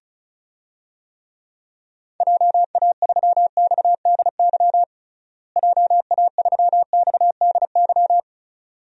Morse code, heard as a keyed tone, about 36 wpm, 700 Hz. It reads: JA3XDY JA3XDY
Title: DXCC-100 Call Signs - 6 Characters - Repeated 2x - 35wpm